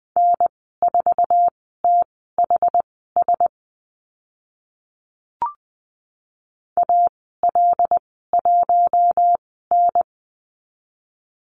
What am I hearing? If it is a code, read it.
Morse code, heard as a keyed tone, about 20 wpm, spaced as for 15 wpm, 700 Hz. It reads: N4THS AL1N
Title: 082 Lesson 14g - Number 2 - Call Signs 20wpm